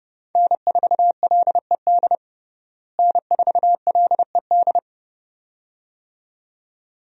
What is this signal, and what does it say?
Morse code, heard as a keyed tone, about 31 wpm, 700 Hz. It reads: N4LED N4LED
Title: USA Call Signs - 5 Characters - Repeated 2x - 30wpm